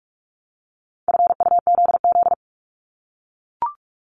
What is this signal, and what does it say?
Morse code, heard as a keyed tone, about 46 wpm, 700 Hz. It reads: FUZZ
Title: Multi-letter Phonemes Repeated 3x 45wpm